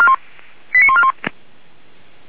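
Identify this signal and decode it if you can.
Five-tone selective call (ZVEI), heard as quick runs of five tones, tones of about 70 ms. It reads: 89141
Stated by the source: ZVEI five-tone selective call received by SDR, ID 89136, channel 464, from the audio archive of Feuerwehr Orlamünde.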